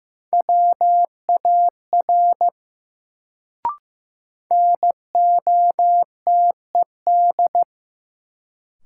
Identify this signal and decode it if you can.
Morse code, heard as a keyed tone, about 15 wpm, 700 Hz. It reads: WAR NOTED
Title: Sets of 5 Words - New Word Drill - 15wpm